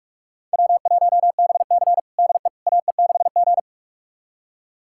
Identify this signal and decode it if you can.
Morse code, heard as a keyed tone, about 46 wpm, 700 Hz. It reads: W1BC DE AE6G
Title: USA Callsign DE Callsign - 4 Characters - 45wpm